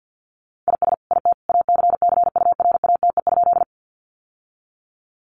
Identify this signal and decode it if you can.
Morse code, heard as a keyed tone, about 48 wpm, 700 Hz. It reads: IS IT ACCURATE?
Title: Sentences from Top 1200 Words 50wpm